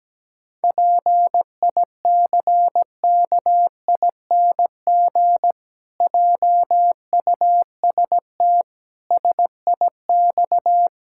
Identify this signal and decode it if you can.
Morse code, heard as a keyed tone, about 17 wpm, 700 Hz. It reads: PICKING JUST SIX
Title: Pangram Challenge v2 - 17wpm